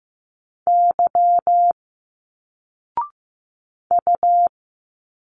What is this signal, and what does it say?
Morse code, heard as a keyed tone, about 15 wpm, 700 Hz. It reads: Y U
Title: Single Letters 15wpm